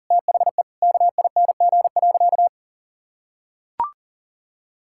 Morse code, heard as a keyed tone, about 40 wpm, 700 Hz. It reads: THE KING.
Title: Sentences from Top 1000 Words - Review Part 6 - 40wpm